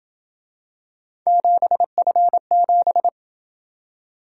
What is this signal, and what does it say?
Morse code, heard as a keyed tone, about 27 wpm, 700 Hz. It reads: 7F7